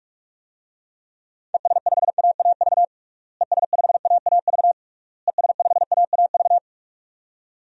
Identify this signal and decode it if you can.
Morse code, heard as a keyed tone, about 46 wpm, 700 Hz. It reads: ES5AAV ES5AAV ES5AAV
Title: DXCC-100 Call Signs - 6 Characters - Repeated 3x - 45wpm